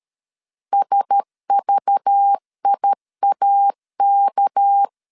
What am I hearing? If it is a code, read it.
Morse code, heard as a keyed tone, about 13 wpm, 800 Hz. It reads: SVIAK
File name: LEZIONI / 6 PKX0 / 6 PKX0 12wpm 2231